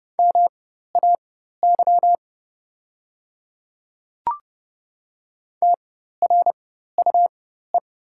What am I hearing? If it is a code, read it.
Morse code, heard as a keyed tone, about 29 wpm, spaced as for 14 wpm, 700 Hz. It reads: MAY TRUE
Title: 4x Character Spacing - Top 500 Words - 30wpm